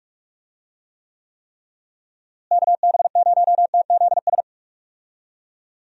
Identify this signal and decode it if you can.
Morse code, heard as a keyed tone, about 46 wpm, 700 Hz. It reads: KB0TZS
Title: USA Call Signs - 6 Characters - Repeated 3x - 45wpm